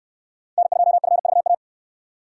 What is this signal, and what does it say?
Morse code, heard as a keyed tone, about 67 wpm, 700 Hz. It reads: N2FVA